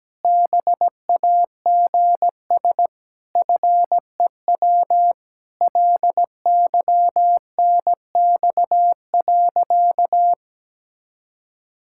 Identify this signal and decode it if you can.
Morse code, heard as a keyed tone, about 17 wpm, 700 Hz. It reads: BAGS FEW LYNX.
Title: Pangram Challenge v2 - 17wpm